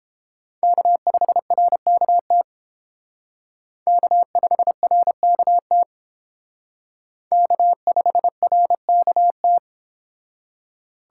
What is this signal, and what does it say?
Morse code, heard as a keyed tone, about 29 wpm, 700 Hz. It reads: K5RKT K5RKT K5RKT